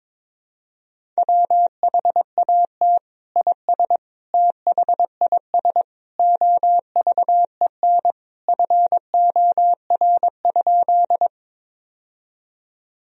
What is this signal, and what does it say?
Morse code, heard as a keyed tone, about 22 wpm, 700 Hz. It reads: WHAT IS THIS OVEN FOR?
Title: Sentences from Top 1000 Words - Review Part 4 - 22wpm